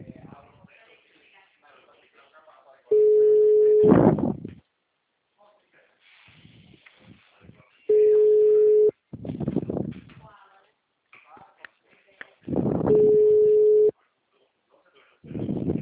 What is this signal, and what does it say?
Ringback tone (Europe, 425 Hz), with speech.